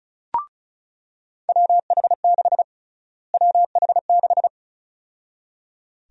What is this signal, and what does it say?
Morse code, heard as a keyed tone, about 36 wpm, 700 Hz. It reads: WH6 WH6
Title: USA Call Sign Prefixes - Repeated 2x - 35wpm